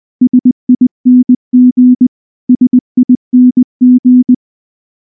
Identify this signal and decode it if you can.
Morse code, heard as a keyed tone, about 19 wpm, 250 Hz. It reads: SING SING
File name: morse-training-20WPM